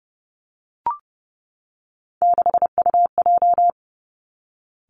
Morse code, heard as a keyed tone, about 30 wpm, 700 Hz. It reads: 6UJ
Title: Three Challenging Letter-Number Combinations 30wpm